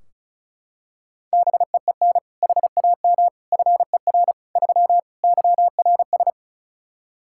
Morse code, heard as a keyed tone, about 34 wpm, 700 Hz. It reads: BEEN HAM FER 3 YRS
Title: K1USN SST QSO Practice 35wpm